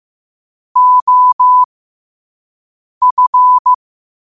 Morse code, heard as a keyed tone, about 15 wpm, spaced as for 5 wpm, 1000 Hz. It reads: OF
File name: qotdfiles / 5wpm